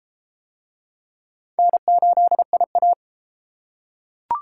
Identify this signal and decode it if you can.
Morse code, heard as a keyed tone, about 30 wpm, 700 Hz. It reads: N8IA